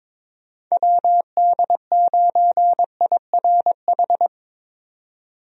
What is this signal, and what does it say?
Morse code, heard as a keyed tone, about 22 wpm, 700 Hz. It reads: WD9IRH
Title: USA Call Signs - 6 Characters - 22wpm